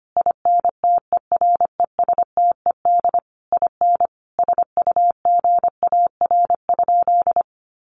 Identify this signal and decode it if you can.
Morse code, heard as a keyed tone, about 25 wpm, 700 Hz. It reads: INTERESTED IN SUGAR?
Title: Sentences from Top 1200 Words 25wpm